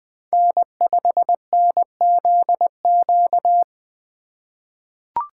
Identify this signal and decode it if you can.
Morse code, heard as a keyed tone, about 20 wpm, 700 Hz. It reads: N5NZQ
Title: USA Call Signs - 5 Characters - 2x Character Spacing - 20wpm